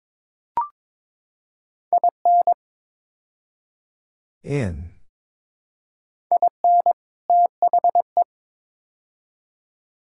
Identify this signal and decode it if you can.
Morse code, heard as a keyed tone, about 22 wpm, 700 Hz. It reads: IN IN THE
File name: practice / 22wpm / Build A Phrase - 5-Word Phrases - 22wpm